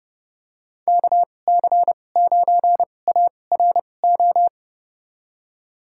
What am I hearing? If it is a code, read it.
Morse code, heard as a keyed tone, about 30 wpm, spaced as for 22 wpm, 700 Hz. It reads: KC9ARO
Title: USA Call Signs - 6 Characters - 2x Character Spacing - 30wpm